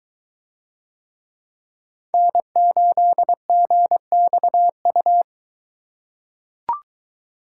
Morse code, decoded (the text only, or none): N8GXU